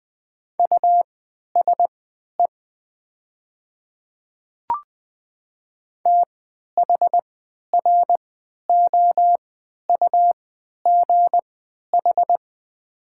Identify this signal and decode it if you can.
Morse code, heard as a keyed tone, about 20 wpm, spaced as for 11 wpm, 700 Hz. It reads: USE THROUGH
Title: 3x Character Spacing - Top 500 Words - 20wpm